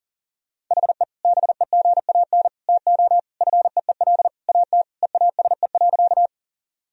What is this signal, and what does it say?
Morse code, heard as a keyed tone, about 40 wpm, 700 Hz. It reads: HE BEGAN TO FEEL AT EASE.
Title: Sentences from Top 1000 Words - Review Part 1 - 40wpm